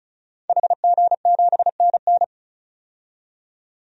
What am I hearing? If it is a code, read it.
Morse code, heard as a keyed tone, about 36 wpm, 700 Hz. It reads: HG7NN
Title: DXCC-100 Call Signs - 5 Characters - 35wpm